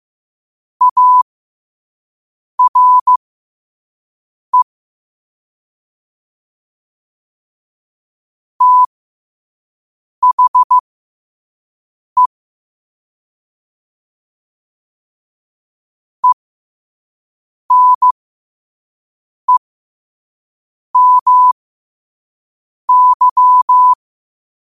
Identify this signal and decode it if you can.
Morse code, heard as a keyed tone, about 15 wpm, spaced as for 5 wpm, 1000 Hz. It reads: ARE THE ENEMY
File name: qotdfiles / 5wpm